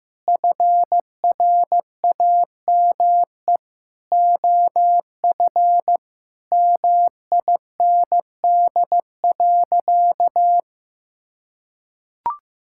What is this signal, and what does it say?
Morse code, heard as a keyed tone, about 15 wpm, 700 Hz. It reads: FRAME OF MIND.
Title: Sentences from Top 1000 Words - Review Part 6 - 15wpm